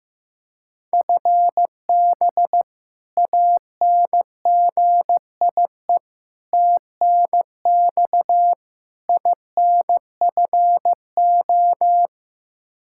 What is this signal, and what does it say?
Morse code, heard as a keyed tone, about 15 wpm, 700 Hz. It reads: FB ANGIE TNX INFO
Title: K1USN SST QSO Practice 15wpm